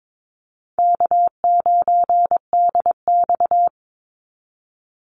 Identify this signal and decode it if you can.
Morse code, heard as a keyed tone, about 22 wpm, 700 Hz. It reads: K9DX
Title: Easy Call Signs Repeated 2x - 22wpm